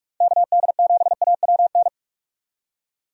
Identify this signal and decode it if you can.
Morse code, heard as a keyed tone, about 42 wpm, 700 Hz. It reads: KD7AWN
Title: USA Call Signs - 6 Characters - 45wpm